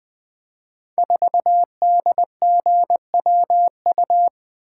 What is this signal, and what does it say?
Morse code, heard as a keyed tone, about 20 wpm, 700 Hz. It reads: 4DGWU
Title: Code Groups - 5 Letter:Numbers - 20wpm